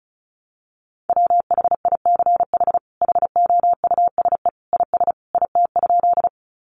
Morse code, heard as a keyed tone, about 34 wpm, 700 Hz. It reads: WHICH HOUSE IS IT?
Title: Sentences from Top 200 Words 35wpm